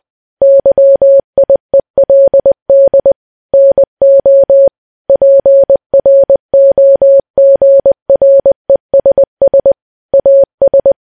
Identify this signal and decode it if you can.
Morse code, heard as a keyed tone, about 20 wpm, 550 Hz. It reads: YIELD NO PROGRESS AS
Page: news-headlines-in-morse-code-at-20-wpm